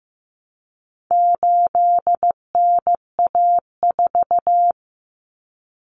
Morse code, heard as a keyed tone, about 15 wpm, 700 Hz. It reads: 8NA4
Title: Common Cut Numbers - 4-Digit 15wpm